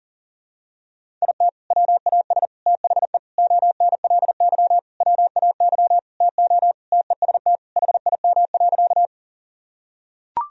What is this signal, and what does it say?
Morse code, heard as a keyed tone, about 37 wpm, 700 Hz. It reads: IT WAS THE ONLY WAY TO TEST HIM.
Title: Sentences from Top 1000 Words - Review Part 3 - 40wpm